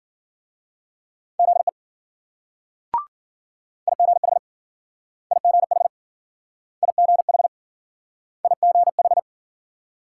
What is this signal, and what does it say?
Morse code, heard as a keyed tone, about 50 wpm, 700 Hz. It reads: ZE IGH IGH IGH IGH
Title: Speed-Racing Multi-letter Phonemes - 40wpm